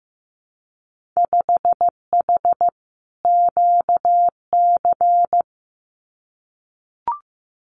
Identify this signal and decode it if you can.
Morse code, heard as a keyed tone, about 15 wpm, 700 Hz. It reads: 5H QC
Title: Winter Field Day Exchanges Encore 15wpm